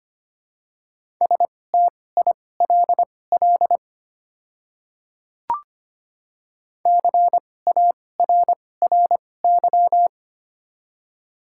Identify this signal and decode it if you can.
Morse code, heard as a keyed tone, about 25 wpm, spaced as for 18 wpm, 700 Hz. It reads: STILL CARRY
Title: 2x Character Spacing - Top 300 Words - 25wpm